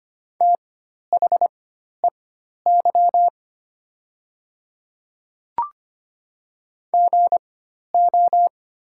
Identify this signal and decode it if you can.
Morse code, heard as a keyed tone, about 25 wpm, spaced as for 12 wpm, 700 Hz. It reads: THEY GO
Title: 4x Character Spacing - Top 300 Words - 25wpm